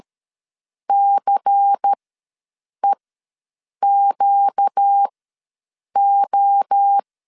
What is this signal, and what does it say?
Morse code, heard as a keyed tone, about 13 wpm, spaced as for 7 wpm, 800 Hz. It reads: CEQO
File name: LEZIONI / 9 2 ws 12wpm 3406